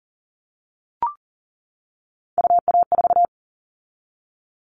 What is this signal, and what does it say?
Morse code, heard as a keyed tone, about 40 wpm, 700 Hz. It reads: UA4
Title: Callsign Prefixes for Top 25 DXCC - 40wpm